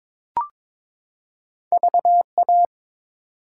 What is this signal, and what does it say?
Morse code, heard as a keyed tone, about 22 wpm, 700 Hz. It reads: VA
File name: ARRL Sections v2 - 22wpm